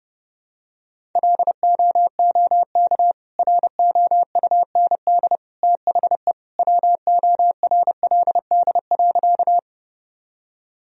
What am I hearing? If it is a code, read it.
Morse code, heard as a keyed tone, about 30 wpm, 700 Hz. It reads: LOOK ROUND THE WORLD.
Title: Sentences from Top 1000 Words - Review Part 1 - 30wpm